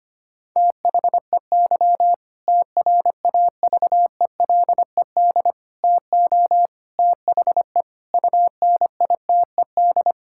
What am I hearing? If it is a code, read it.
Morse code, heard as a keyed tone, about 25 wpm, 700 Hz. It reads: THEY TRAVELED TO THE UNITED
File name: practice / 25wpm / Sentences with 2-Word Phrases - 25wpm